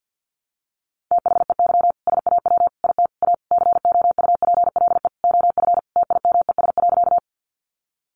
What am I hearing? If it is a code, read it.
Morse code, heard as a keyed tone, about 50 wpm, 700 Hz. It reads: THEY SAW IT A COUPLE OF TIMES.
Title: Sentences from Top 1100 Words 50wpm